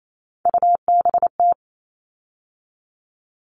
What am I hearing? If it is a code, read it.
Morse code, heard as a keyed tone, about 28 wpm, 700 Hz. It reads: UBT
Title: All Cut Numbers - 3-Digit 28wpm